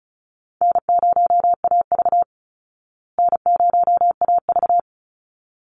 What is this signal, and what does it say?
Morse code, heard as a keyed tone, about 34 wpm, 700 Hz. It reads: N0AV N0AV
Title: Easy Call Signs Repeated 2x - 35wpm